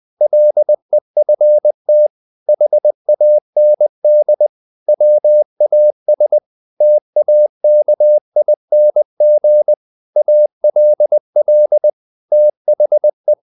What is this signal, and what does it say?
Morse code, heard as a keyed tone, about 20 wpm, 600 Hz. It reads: LEFT HAND WAS TAKING ALL THE
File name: the old man and the sea part 2 20wpm